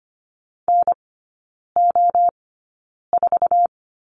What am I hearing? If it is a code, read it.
Morse code, heard as a keyed tone, about 25 wpm, spaced as for 9 wpm, 690 Hz. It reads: NO4
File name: Lesson 14e - Number 2 - Three Characters 25wpm